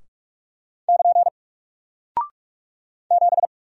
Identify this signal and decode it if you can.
Morse code, heard as a keyed tone, about 44 wpm, 700 Hz. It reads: <KN> 7
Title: Single Character (Letter:Number:Punctuation:Prosign) Alt 45wpm